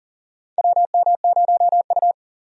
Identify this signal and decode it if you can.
Morse code, heard as a keyed tone, about 40 wpm, 700 Hz. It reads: WM0U
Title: USA Call Signs - 4 Characters - 40wpm